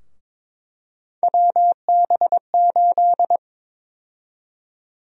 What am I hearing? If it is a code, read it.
Morse code, heard as a keyed tone, about 22 wpm, 700 Hz. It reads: WB8
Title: USA Call Sign Prefixes - Repeated 2x - 22wpm